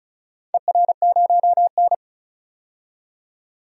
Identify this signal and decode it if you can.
Morse code, heard as a keyed tone, about 35 wpm, 700 Hz. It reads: ER0N
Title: DXCC-100 Call Signs - 4 Characters - 35wpm